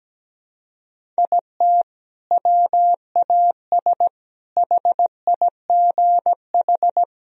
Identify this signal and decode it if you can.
Morse code, heard as a keyed tone, about 17 wpm, 700 Hz. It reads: IT WAS HIGH
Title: Sentences from Top 1000 Words - Review Part 1 - 17wpm